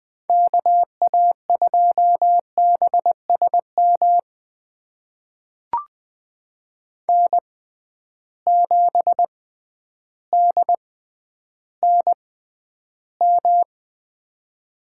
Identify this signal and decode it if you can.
Morse code, heard as a keyed tone, about 20 wpm, spaced as for 11 wpm, 700 Hz. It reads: KA2BSM N7DNM
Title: USA Call Signs - 6x Character Spacing - 20wpm